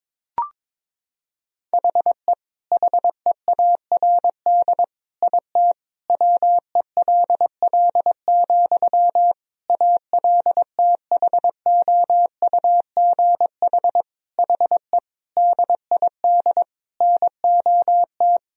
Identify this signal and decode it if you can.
Morse code, heard as a keyed tone, about 22 wpm, 700 Hz. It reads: HE HEARD IT WELL, ALTHOUGH HE DID NOT
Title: Sentences from Top 1000 Words - Review Part 3 - 22wpm